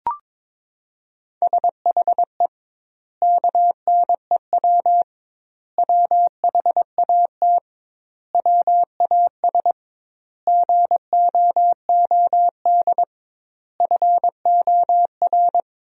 Morse code, heard as a keyed tone, about 22 wpm, 700 Hz. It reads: SHE KNEW WHAT WAS GOOD FOR
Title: 2x Word-Spacing - Sentences from Top 400 Words - 22wpm